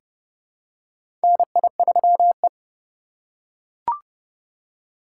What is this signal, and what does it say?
Morse code, heard as a keyed tone, about 30 wpm, 700 Hz. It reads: NI3E
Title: USA Call Signs - 4 Characters - 4x Character Spacing - 30wpm